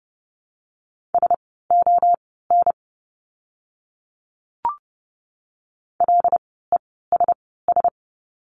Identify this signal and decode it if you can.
Morse code, heard as a keyed tone, about 30 wpm, spaced as for 17 wpm, 700 Hz. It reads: SON LESS
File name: Lesson 17f - Letter u - Words 30wpm